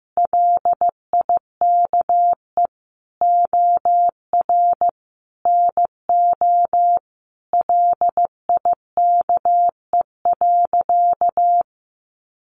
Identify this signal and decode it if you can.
Morse code, heard as a keyed tone, about 15 wpm, 700 Hz. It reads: LIKE OR NO LIKE.
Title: Sentences from Top 1000 Words 15wpm